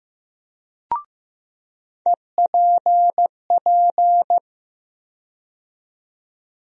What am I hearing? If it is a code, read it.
Morse code, heard as a keyed tone, about 15 wpm, 700 Hz. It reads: EPP
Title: USA Call Sign Postfixes - 3 Letters 15wpm